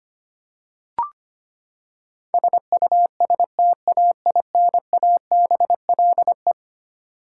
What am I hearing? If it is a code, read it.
Morse code, heard as a keyed tone, about 25 wpm, 700 Hz. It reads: SUSTAINABLE
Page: morse-code-ninja-practice-25wpm